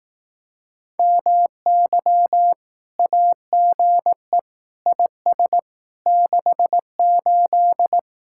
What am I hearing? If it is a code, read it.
Morse code, heard as a keyed tone, about 18 wpm, 700 Hz. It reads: MY AGE IS 68
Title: Speed-Racing Common QSO Phrases Encore - 15wpm